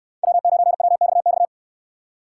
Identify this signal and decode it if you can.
Morse code, heard as a keyed tone, about 69 wpm, 700 Hz. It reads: W9WXX